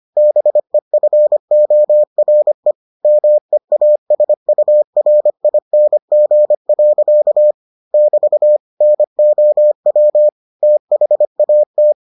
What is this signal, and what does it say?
Morse code, heard as a keyed tone, about 25 wpm, 600 Hz. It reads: BEFORE MEASURING. <BT> NOW THAT